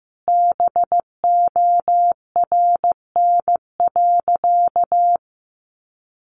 Morse code, heard as a keyed tone, about 15 wpm, 700 Hz. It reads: BORN.
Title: Sentences from Top 1300 Words 15wpm